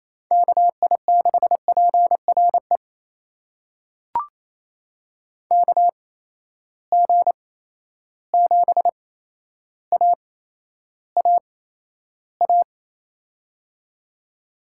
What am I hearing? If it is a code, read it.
Morse code, heard as a keyed tone, about 28 wpm, spaced as for 12 wpm, 700 Hz. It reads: KI6PRE KG7AAA
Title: USA Call Signs - 6 Characters - 8x Character Spacing - 28wpm